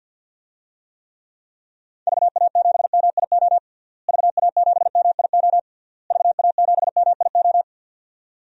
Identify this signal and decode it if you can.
Morse code, heard as a keyed tone, about 52 wpm, 700 Hz. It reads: VA7MIO VA7MIO VA7MIO
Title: DXCC-100 Call Signs - 6 Characters - Repeated 3x - 50wpm